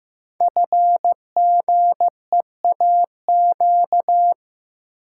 Morse code, heard as a keyed tone, about 15 wpm, 700 Hz. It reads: FGEAQ
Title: Code Groups - 5 Letters - 15wpm